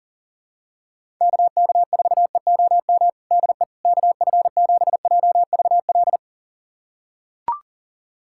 Morse code, heard as a keyed tone, about 40 wpm, 700 Hz. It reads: KK4EOM DE KF7JVL
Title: USA Callsign DE Callsign - 6 Characters - 40wpm